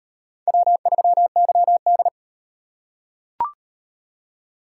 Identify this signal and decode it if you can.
Morse code, heard as a keyed tone, about 34 wpm, 700 Hz. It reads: W3YD